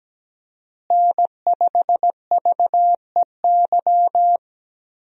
Morse code, heard as a keyed tone, about 17 wpm, 700 Hz. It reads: N5VEY
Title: USA Call Signs - 5 Characters - Repeated 3x - 17wpm